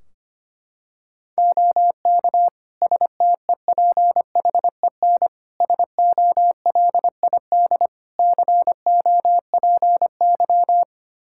Morse code, heard as a keyed tone, about 25 wpm, 700 Hz. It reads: OK STEPHEN SOLID COPY
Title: K1USN SST QSO Practice 25wpm